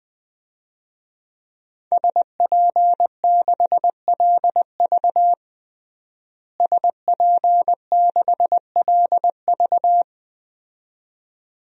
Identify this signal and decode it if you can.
Morse code, heard as a keyed tone, about 19 wpm, 700 Hz. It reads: SP6LV SP6LV
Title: DXCC-100 Call Signs - 5 Characters - Repeated 2x - 20wpm